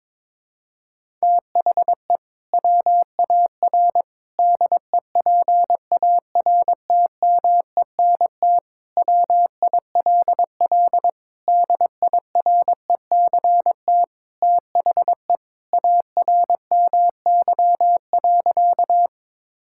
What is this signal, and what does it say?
Morse code, heard as a keyed tone, about 22 wpm, 700 Hz. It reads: THE WAR DEPARTMENT WILL DIRECT THE ARMY.
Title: Sentences from Top 1000 Words - Review Part 5 - 22wpm